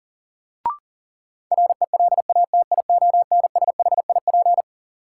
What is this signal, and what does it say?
Morse code, heard as a keyed tone, about 40 wpm, 700 Hz. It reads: RELATIONSHIP
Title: Sentences from Top 1100 Words - New Word Drill - 40wpm